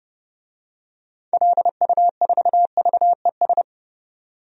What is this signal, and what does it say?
Morse code, heard as a keyed tone, about 30 wpm, 700 Hz. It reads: LU4VES
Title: DXCC-100 Call Signs - 6 Characters - 30wpm